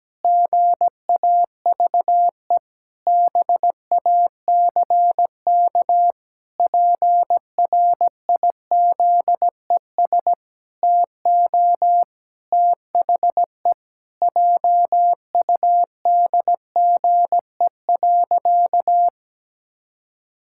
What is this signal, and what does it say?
Morse code, heard as a keyed tone, about 17 wpm, 700 Hz. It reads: GAVE BACK PRIZES TO THE JUDGE.
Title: Pangram Challenge v2 - 17wpm